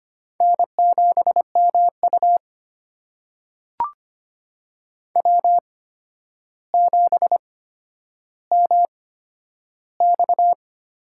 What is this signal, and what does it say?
Morse code, heard as a keyed tone, about 25 wpm, spaced as for 11 wpm, 700 Hz. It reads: N7MU W7MX
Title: USA Call Signs - 4 Characters - 8x Character Spacing - 25wpm